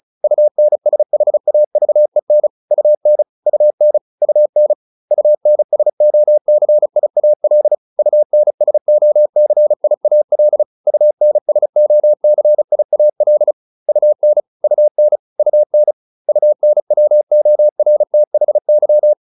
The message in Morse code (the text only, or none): UNSHAVEN UN UN UN UNSOCIAL UNSOCIAL UNSOCIAL UN UN UN UNWORTHY